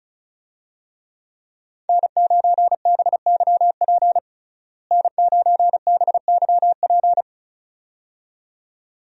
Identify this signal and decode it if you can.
Morse code, heard as a keyed tone, about 34 wpm, 700 Hz. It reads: N9BYP N9BYP
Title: USA Call Signs - 5 Characters - Repeated 2x - 35wpm